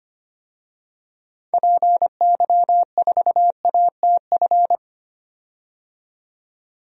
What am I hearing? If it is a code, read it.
Morse code, heard as a keyed tone, about 25 wpm, 690 Hz. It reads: PY4ATF
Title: DXCC-100 Call Signs - 6 Characters - 25wpm